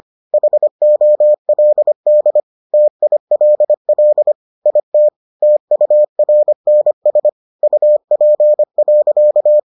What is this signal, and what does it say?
Morse code, heard as a keyed tone, about 25 wpm, 600 Hz. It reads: HOLD TILL IT TURNS UP.